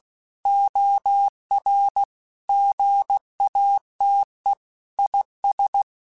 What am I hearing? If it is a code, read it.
Morse code, heard as a keyed tone, about 16 wpm, 800 Hz. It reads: OR GATE IS
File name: Technical Subjects / Technical Subjects-mp3s / AND Gate @15